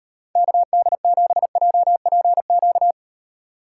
Morse code, heard as a keyed tone, about 38 wpm, 700 Hz. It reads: KD7JPQ